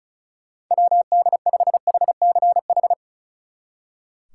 Morse code, heard as a keyed tone, about 44 wpm, 700 Hz. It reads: WD5HCH